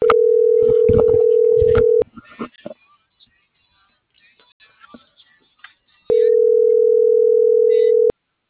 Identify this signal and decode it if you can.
Ringback tone (North America), 440 + 480 Hz.